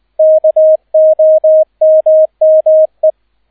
Morse code, heard as a keyed tone, about 20 wpm, 630 Hz. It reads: KOMME
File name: 002 Zweiter Streich 20wpm